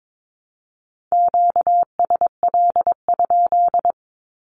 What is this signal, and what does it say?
Morse code, heard as a keyed tone, about 22 wpm, 700 Hz. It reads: QSL?
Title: Q-Codes Repeated 3x 22wpm